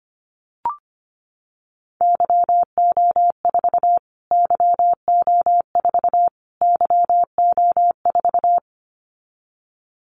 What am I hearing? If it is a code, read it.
Morse code, heard as a keyed tone, about 25 wpm, 700 Hz. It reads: YO4 YO4 YO4
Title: Callsign Prefixes for Top 25 DXCC - Repeated 3x - 25wpm